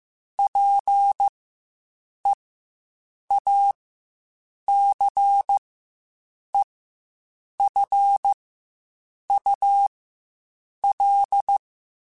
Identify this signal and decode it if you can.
Morse code, heard as a keyed tone, about 15 wpm, spaced as for 7 wpm, 800 Hz. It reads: PEACEFUL
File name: Quotes Spaced / Quotes Spaced-mp3s / ChildAsleep sp @10